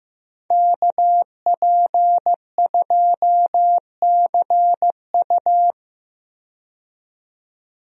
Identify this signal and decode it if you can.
Morse code, heard as a keyed tone, about 15 wpm, 700 Hz. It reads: KP2CU
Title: DXCC-100 Call Signs - 5 Characters - 15wpm